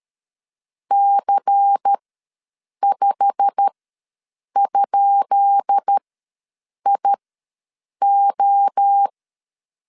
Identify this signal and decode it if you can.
Morse code, heard as a keyed tone, about 13 wpm, spaced as for 7 wpm, 800 Hz. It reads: C5?IO
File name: LEZIONI / 10 SIGNS / 10B INT ws 12wpm 3527